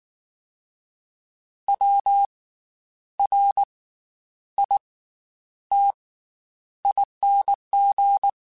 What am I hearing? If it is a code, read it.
Morse code, heard as a keyed tone, about 19 wpm, 800 Hz. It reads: W R I T ING
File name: Special INGs / Special INGs-mp3s / Slow ing @18